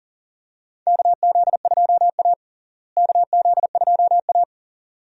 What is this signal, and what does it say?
Morse code, heard as a keyed tone, about 40 wpm, 700 Hz. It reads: KZ2A KZ2A
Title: USA Call Signs - 4 Characters - Repeated 2x - 40wpm